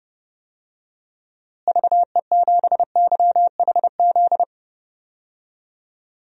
Morse code, heard as a keyed tone, about 30 wpm, 700 Hz. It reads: VE7YHZ
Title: DXCC-100 Call Signs - 6 Characters - Repeated 3x - 30wpm